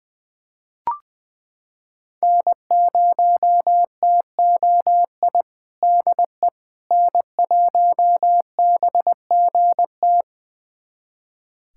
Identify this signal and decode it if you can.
Morse code, heard as a keyed tone, about 20 wpm, 700 Hz. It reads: N0TOI DE N1BGT
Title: USA Callsign DE Callsign - 5 Characters - 20wpm